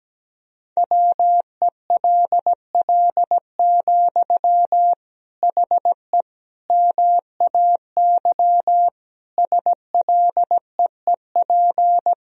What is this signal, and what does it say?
Morse code, heard as a keyed tone, about 17 wpm, 700 Hz. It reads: WELL, HE MAY SLEEP
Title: Sentences from Top 1000 Words - Review Part 4 - 17wpm